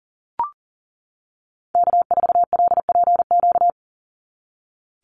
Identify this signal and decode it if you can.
Morse code, heard as a keyed tone, about 38 wpm, 700 Hz. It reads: K4LPQ
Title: NA SOTA Activator Callsigns 40wpm